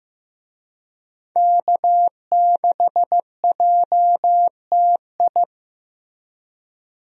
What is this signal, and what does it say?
Morse code, heard as a keyed tone, about 15 wpm, 700 Hz. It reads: K6JTI
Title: USA Call Signs - 5 Characters - Repeated 3x - 15wpm